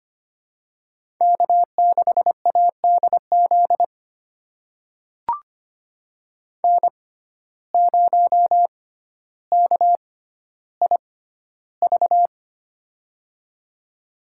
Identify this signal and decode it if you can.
Morse code, heard as a keyed tone, about 25 wpm, spaced as for 13 wpm, 700 Hz. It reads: K6ADZ N0KIV
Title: USA Call Signs - 5 Characters - 6x Character Spacing - 25wpm